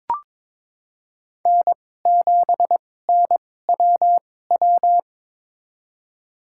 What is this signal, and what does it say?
Morse code, heard as a keyed tone, about 22 wpm, spaced as for 16 wpm, 700 Hz. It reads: N7NWW
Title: USA Call Signs - 5 Characters - 2x Character Spacing - 22wpm